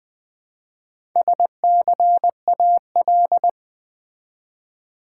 Morse code, heard as a keyed tone, about 20 wpm, 700 Hz. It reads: SCAL
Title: Build A Word - 1 Letter At A Time - 5-Letter Words - 20wpm